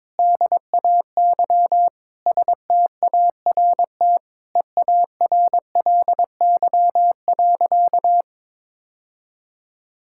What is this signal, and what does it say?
Morse code, heard as a keyed tone, about 22 wpm, 700 Hz. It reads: DAY START EARLY.
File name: Sentences with 7-Word Phrases - 22wpm